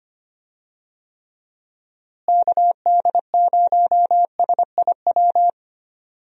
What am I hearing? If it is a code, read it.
Morse code, heard as a keyed tone, about 25 wpm, 700 Hz. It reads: KD0SIW
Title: USA Call Signs - 6 Characters - Repeated 2x - 25wpm